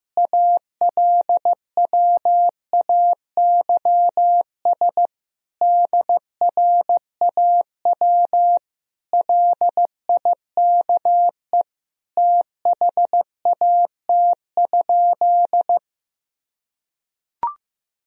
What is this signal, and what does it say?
Morse code, heard as a keyed tone, about 15 wpm, 700 Hz. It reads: ALWAYS DRAW LIKE THAT?
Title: Sentences from Top 1000 Words - Review Part 1 - 15wpm